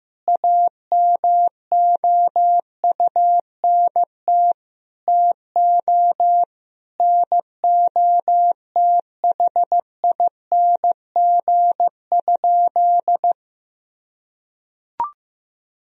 Morse code, heard as a keyed tone, about 15 wpm, 700 Hz. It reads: AMOUNT TO NOTHING?
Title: Sentences from Top 1000 Words - Review Part 4 - 15wpm